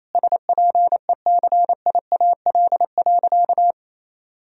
Morse code, heard as a keyed tone, about 28 wpm, 700 Hz. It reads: SPECIAL.
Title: Sentences from Top 1000 Words - Review Part 3 - 28wpm